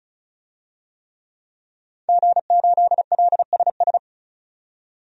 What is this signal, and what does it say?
Morse code, heard as a keyed tone, about 36 wpm, 700 Hz. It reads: G8LSS
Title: DXCC-100 Call Signs - 5 Characters - 35wpm